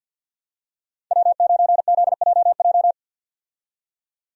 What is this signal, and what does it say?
Morse code, heard as a keyed tone, about 51 wpm, 700 Hz. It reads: W9ZJJ
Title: USA Call Signs - 5 Characters - 50wpm